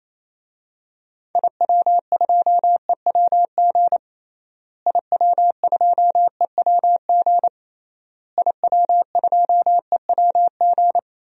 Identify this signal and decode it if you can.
Morse code, heard as a keyed tone, about 28 wpm, 700 Hz. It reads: IW2EWG IW2EWG IW2EWG
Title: DXCC-100 Call Signs - 6 Characters - Repeated 3x - 28wpm